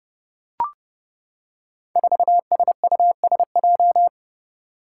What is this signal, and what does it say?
Morse code, heard as a keyed tone, about 30 wpm, 700 Hz. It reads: 4SUSJ
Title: Five Challenging Letter-Number Combinations 30wpm